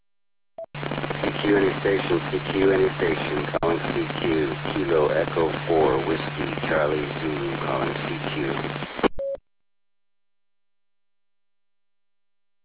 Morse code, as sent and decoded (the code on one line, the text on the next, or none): none